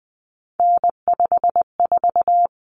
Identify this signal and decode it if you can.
Morse code, heard as a keyed tone, about 20 wpm, 700 Hz. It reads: N54
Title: Common Cut Numbers - 3-Digit 20wpm